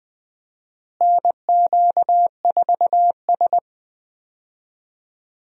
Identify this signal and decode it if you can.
Morse code, heard as a keyed tone, about 20 wpm, 700 Hz. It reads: NQ4S
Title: USA Call Signs - 4 Characters - Repeated 2x - 20wpm